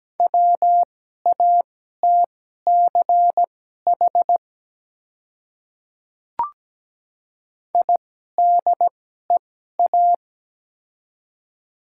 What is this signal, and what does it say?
Morse code, heard as a keyed tone, about 17 wpm, spaced as for 12 wpm, 700 Hz. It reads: WATCH IDEA
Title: 2x Character Spacing - Top 300 Words - 17wpm